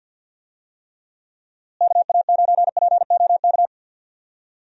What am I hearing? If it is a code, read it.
Morse code, heard as a keyed tone, about 47 wpm, 700 Hz. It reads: KA9POK
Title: USA Call Signs - 6 Characters - Repeated 3x - 50wpm